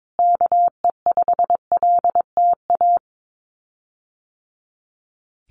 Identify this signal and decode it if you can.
Morse code, heard as a keyed tone, about 22 wpm, 700 Hz. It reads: KE5LTA
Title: USA Call Signs Repeated 3x 22wpm